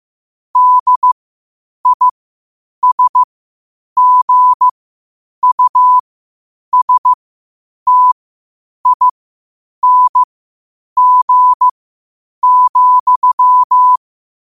Morse code, heard as a keyed tone, about 15 wpm, spaced as for 8 wpm, 1000 Hz. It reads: DISGUSTING,
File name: qotdfiles / 8wpm